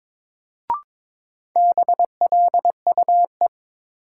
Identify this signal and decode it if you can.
Morse code, heard as a keyed tone, about 21 wpm, 700 Hz. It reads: BLUE
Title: Sentences from Top 1200 Words - New Word Drill Alt - 22wpm